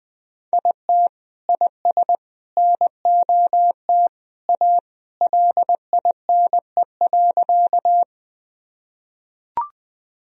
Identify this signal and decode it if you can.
Morse code, heard as a keyed tone, about 21 wpm, 700 Hz. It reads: IT IS NOT A LINE.
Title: Sentences from Top 1000 Words - Review Part 1 - 20wpm